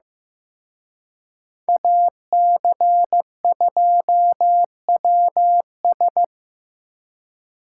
Morse code, heard as a keyed tone, about 18 wpm, 700 Hz. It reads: AC2WS
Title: Speed Racing - USA 2x2 Call Signs 15wpm